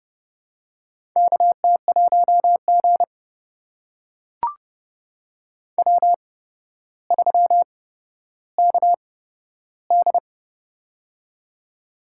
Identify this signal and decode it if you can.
Morse code, heard as a keyed tone, about 30 wpm, spaced as for 13 wpm, 700 Hz. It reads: KT1G W3KD
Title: USA Call Signs - 4 Characters - 8x Character Spacing - 30wpm